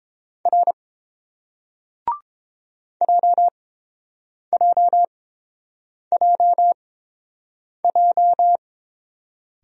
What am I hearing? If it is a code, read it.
Morse code, heard as a keyed tone, about 30 wpm, spaced as for 7 wpm, 690 Hz. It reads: R JJJJ